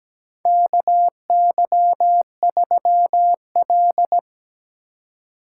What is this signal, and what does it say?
Morse code, heard as a keyed tone, about 17 wpm, 700 Hz. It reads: KY3L
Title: USA Call Signs - 4 Characters - 3x Character Spacing - 17wpm